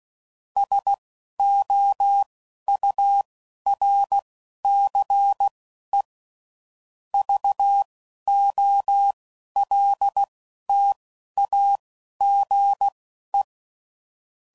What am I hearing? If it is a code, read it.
Morse code, heard as a keyed tone, about 16 wpm, spaced as for 11 wpm, 800 Hz. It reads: SOURCE VOLTAGE
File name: CW Code Groups @10WPM / Technical Subjects / XC